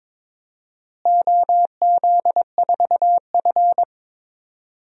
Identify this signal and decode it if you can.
Morse code, heard as a keyed tone, about 22 wpm, 700 Hz. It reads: OZ4F
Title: DXCC-100 Call Signs - 4 Characters - Repeated 3x - 22wpm